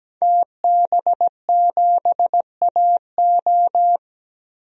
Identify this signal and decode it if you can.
Morse code, heard as a keyed tone, about 17 wpm, 700 Hz. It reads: TB7AO
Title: DXCC-100 Call Signs - 5 Characters - 17wpm